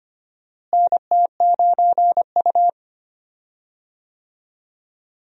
Morse code, heard as a keyed tone, about 25 wpm, 700 Hz. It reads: NT9U
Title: USA Call Signs - 4 Characters - Repeated 3x - 25wpm